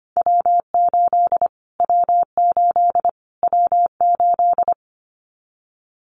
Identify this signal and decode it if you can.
Morse code, heard as a keyed tone, about 25 wpm, 700 Hz. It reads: W8 W8 W8
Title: Callsign Prefixes for Top 25 DXCC - Repeated 3x - 25wpm